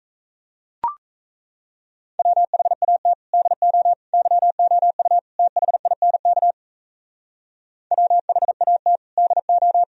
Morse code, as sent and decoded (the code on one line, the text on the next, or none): .-- .... .- - / -.. --- / -.-- --- ..- / - .... .. -. -.- / .-- .... .- - / -.. ---
WHAT DO YOU THINK WHAT DO